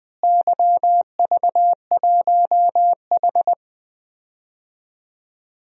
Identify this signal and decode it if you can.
Morse code, heard as a keyed tone, about 20 wpm, 700 Hz. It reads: YV1H
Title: DXCC-100 Call Signs - 4 Characters - Repeated 2x - 20wpm